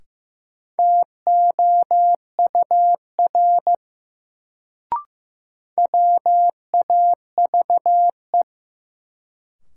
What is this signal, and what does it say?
Morse code, heard as a keyed tone, about 15 wpm, 700 Hz. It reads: TOUR WAVE
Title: Sentences from Top 1400 Words - New Word Drill - 15wpm